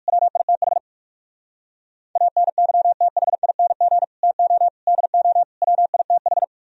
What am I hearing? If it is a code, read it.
Morse code, heard as a keyed tone, about 58 wpm, 700 Hz. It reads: WITH ANYTHING TO DO WITH